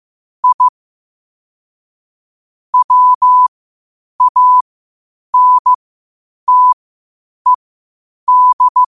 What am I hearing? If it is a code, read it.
Morse code, heard as a keyed tone, about 15 wpm, spaced as for 8 wpm, 1000 Hz. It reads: I WANTED
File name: qotdfiles / 8wpm